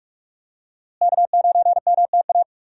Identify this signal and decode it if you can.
Morse code, heard as a keyed tone, about 45 wpm, 700 Hz. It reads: K9MTA
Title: USA Call Signs - 5 Characters - Repeated 2x - 45wpm